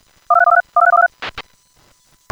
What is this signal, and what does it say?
DTMF, heard as telephone touch tones, tones of about 60 ms. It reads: 123123123123